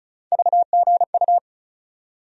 Morse code, heard as a keyed tone, about 34 wpm, 700 Hz. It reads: VGU
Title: USA Call Sign Postfixes - 3 Letters 35wpm